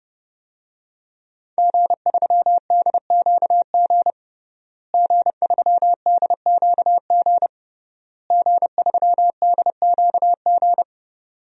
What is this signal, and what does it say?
Morse code, heard as a keyed tone, about 30 wpm, 700 Hz. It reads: G3DQG G3DQG G3DQG
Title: DXCC-100 Call Signs - 5 Characters - Repeated 3x - 30wpm